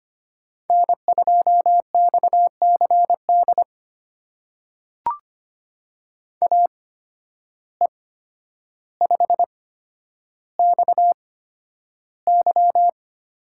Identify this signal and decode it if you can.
Morse code, heard as a keyed tone, about 25 wpm, spaced as for 11 wpm, 700 Hz. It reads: N2XCD AE5XY
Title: USA Call Signs - 5 Characters - 8x Character Spacing - 25wpm